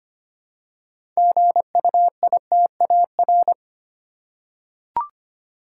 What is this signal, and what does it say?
Morse code, heard as a keyed tone, about 25 wpm, 690 Hz. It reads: GUITAR
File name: practice / 25wpm / Easy Words with Hints - 25wpm